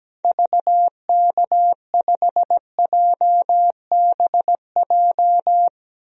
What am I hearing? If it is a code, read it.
Morse code, heard as a keyed tone, about 17 wpm, 700 Hz. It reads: VK5JBJ
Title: DXCC-100 Call Signs - 6 Characters - 17wpm